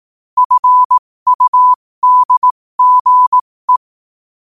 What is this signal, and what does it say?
Morse code, heard as a keyed tone, about 18 wpm, spaced as for 15 wpm, 1000 Hz. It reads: FUDGE
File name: qotdfiles / 15wpm